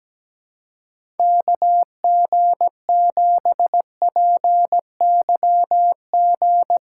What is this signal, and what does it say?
Morse code, heard as a keyed tone, about 17 wpm, 700 Hz. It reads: KG7PYG
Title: USA Call Signs - 3x Character Spacing - 17wpm